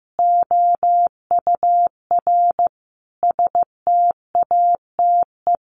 Morse code, heard as a keyed tone, about 15 wpm, 700 Hz. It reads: OUR STATE
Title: Sentences from Top 1400 Words 15wpm